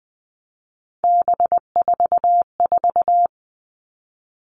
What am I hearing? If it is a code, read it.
Morse code, heard as a keyed tone, about 20 wpm, 700 Hz. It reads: B44
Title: All Cut Numbers - 3-Digit 20wpm